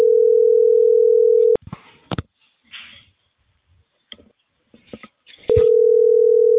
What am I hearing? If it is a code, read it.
Ringback tone (North America), 440 + 480 Hz.